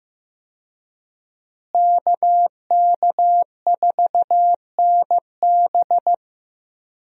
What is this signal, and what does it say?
Morse code, heard as a keyed tone, about 15 wpm, 700 Hz. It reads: KK4NB
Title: USA Call Signs - 5 Characters - Repeated 3x - 15wpm